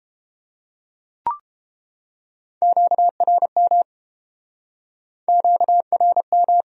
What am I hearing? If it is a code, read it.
Morse code, heard as a keyed tone, about 32 wpm, 700 Hz. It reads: QRM QRM